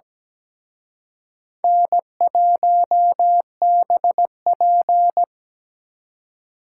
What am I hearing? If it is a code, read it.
Morse code, heard as a keyed tone, about 20 wpm, 700 Hz. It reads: N1BP
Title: Speed Racing - USA 1x2 Call Signs 17wpm